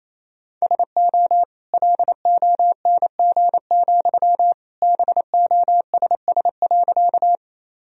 Morse code, heard as a keyed tone, about 28 wpm, 700 Hz. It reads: SO LONG, BOSS.
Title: Sentences from Top 1000 Words - Review Part 4 - 28wpm